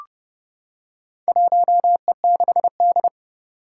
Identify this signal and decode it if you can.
Morse code, heard as a keyed tone, about 30 wpm, 700 Hz. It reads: 1E6D
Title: Four Challenging Letter-Number Combinations 30wpm